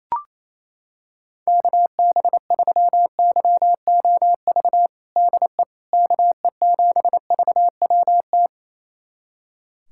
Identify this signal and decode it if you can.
Morse code, heard as a keyed tone, about 28 wpm, 700 Hz. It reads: KB3YOV DE KE7VWT
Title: USA Callsign DE Callsign - 6 Characters - 28wpm